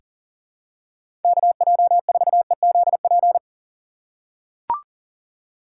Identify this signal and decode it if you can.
Morse code, heard as a keyed tone, about 41 wpm, 690 Hz. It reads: KJ4EZP
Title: USA Call Signs - 6 Characters - 2x Character Spacing - 40wpm